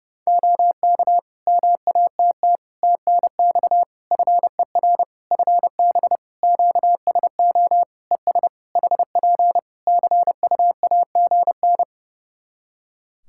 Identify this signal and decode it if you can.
Morse code, heard as a keyed tone, about 30 wpm, 700 Hz. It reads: OK MATT TNX FER FB QSO ES HP CUAGN
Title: Practice QSO - Exchange 4 - Ending - 30wpm